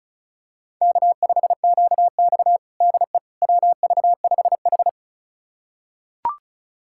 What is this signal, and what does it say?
Morse code, heard as a keyed tone, about 34 wpm, 700 Hz. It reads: K5QX DE WV5H
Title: USA Callsign DE Callsign - 4 Characters - 35wpm